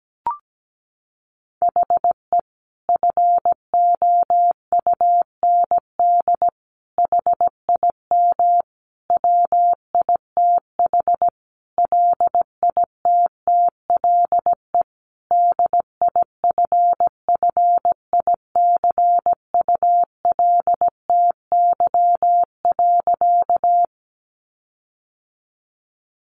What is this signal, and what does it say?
Morse code, heard as a keyed tone, about 17 wpm, 700 Hz. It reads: HE FOUND HIM WITH LITTLE DIFFICULTY.
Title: Sentences from Top 1500 Words 17wpm